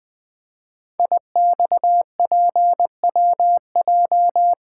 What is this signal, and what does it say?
Morse code, heard as a keyed tone, about 20 wpm, 700 Hz. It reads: IXPWJ
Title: Code Groups - 5 Letters - 20wpm